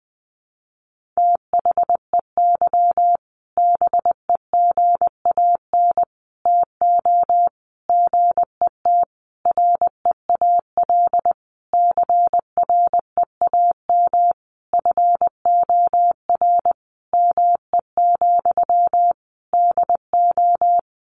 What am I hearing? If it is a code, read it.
Morse code, heard as a keyed tone, about 20 wpm, 700 Hz. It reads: THEY BEGAN TO GET REAL CREAM FOR ME, DO
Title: Sentences from Top 1500 Words 20wpm